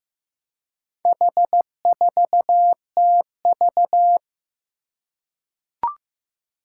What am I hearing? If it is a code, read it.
Morse code, heard as a keyed tone, about 15 wpm, 700 Hz. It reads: H4TV
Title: Four Challenging Letter-Number Combinations 15wpm